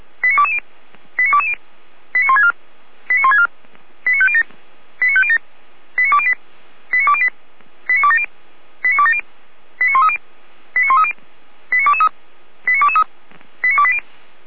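Five-tone selective call (ZVEI), heard as quick runs of five tones, tones of about 70 ms. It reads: 89220 89220 89164 89164 89507 89507 89208 89208 89270 89270 89130 89130 89223 89223 89280
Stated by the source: ZVEI five-tone call 89050 received by SDR.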